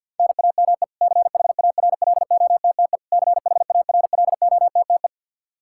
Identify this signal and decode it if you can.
Morse code, heard as a keyed tone, about 52 wpm, 700 Hz. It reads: NAME CHARLOTTE CHARLOTTE
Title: Practice QSO - Exchange 1 - Beginning - 50wpm